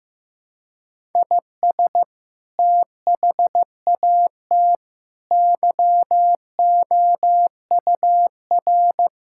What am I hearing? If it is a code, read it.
Morse code, heard as a keyed tone, about 16 wpm, 700 Hz. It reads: IS THAT YOUR
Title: Sentences from Top 1600 Words 15wpm